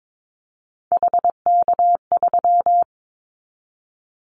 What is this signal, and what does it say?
Morse code, heard as a keyed tone, about 22 wpm, 700 Hz. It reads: HK3
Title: Callsign Prefixes for Top 100 DXCC - 22wpm